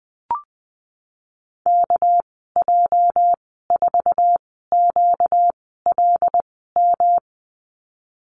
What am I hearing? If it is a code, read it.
Morse code, heard as a keyed tone, about 19 wpm, spaced as for 14 wpm, 700 Hz. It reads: KJ4QLM
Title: 238 Lesson 38g - Letter q - Call Signs 20wpm